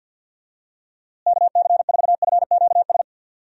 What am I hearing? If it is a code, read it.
Morse code, heard as a keyed tone, about 51 wpm, 700 Hz. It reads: KC4FQS
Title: USA Call Signs - 6 Characters - Repeated 3x - 50wpm